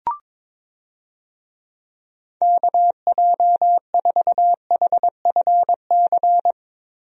Morse code, heard as a keyed tone, about 22 wpm, 700 Hz. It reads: KJ4HFC
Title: USA Call Signs - 6 Characters - Repeated 3x - 22wpm